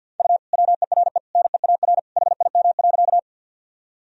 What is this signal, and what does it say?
Morse code, heard as a keyed tone, about 50 wpm, 700 Hz. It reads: U WERE NEAR HIM.
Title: Sentences from Top 200 Words and Abbreviations 50wpm